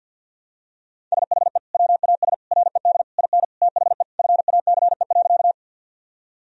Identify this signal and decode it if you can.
Morse code, heard as a keyed tone, about 50 wpm, 700 Hz. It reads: SHE WAS RED IN THE FACE.
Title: Sentences from Top 1000 Words - Review Part 2 - 50wpm